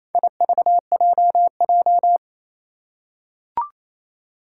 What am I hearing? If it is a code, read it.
Morse code, heard as a keyed tone, about 28 wpm, 700 Hz. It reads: IVJJ
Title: Four Challenging Letter-Number Combinations 28wpm